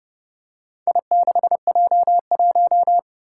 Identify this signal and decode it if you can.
Morse code, heard as a keyed tone, about 30 wpm, 700 Hz. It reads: I6J1
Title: Four Challenging Letter-Number Combinations 30wpm